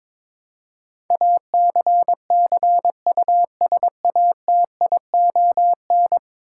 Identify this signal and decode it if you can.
Morse code, heard as a keyed tone, about 22 wpm, 700 Hz. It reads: ACCUSATION
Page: morse-code-ninja-practice-22wpm